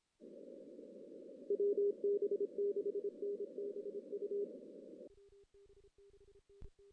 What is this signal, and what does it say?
Morse code, heard as a keyed tone, about 26 wpm, 400 Hz. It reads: WB6NBU
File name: 2016 JAN CW NAQP / 2016 Jan naqp k5ot - 0-6